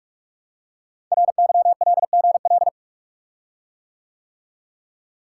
Morse code, heard as a keyed tone, about 44 wpm, 690 Hz. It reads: RYLGL
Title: Code Groups - 5 Letters - 45wpm